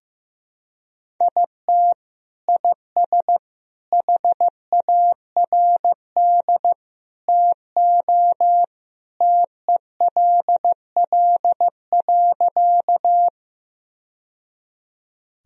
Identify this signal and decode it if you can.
Morse code, heard as a keyed tone, about 15 wpm, 700 Hz. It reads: IT IS HARD TO TELL.
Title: Sentences from Top 1000 Words - Review Part 1 - 15wpm